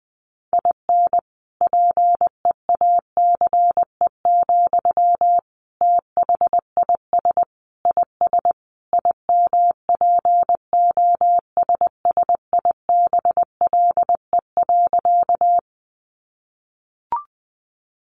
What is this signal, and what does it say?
Morse code, heard as a keyed tone, about 20 wpm, 700 Hz. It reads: IN PEACE, THIS IS IMPOSSIBLE.
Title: Sentences from Top 1500 Words 20wpm